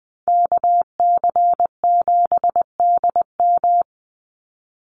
Morse code, heard as a keyed tone, about 20 wpm, 700 Hz. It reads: KC7DM
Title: NA SOTA Activator Callsigns 20wpm